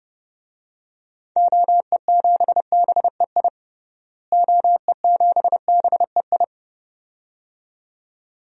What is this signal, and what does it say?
Morse code, heard as a keyed tone, about 32 wpm, 700 Hz. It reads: OE7BEI OE7BEI
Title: DXCC-100 Call Signs - 6 Characters - Repeated 2x - 30wpm